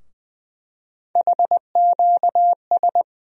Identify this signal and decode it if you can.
Morse code, heard as a keyed tone, about 20 wpm, 700 Hz. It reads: HQS
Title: USA Call Sign Postfixes - 3 Letters 20wpm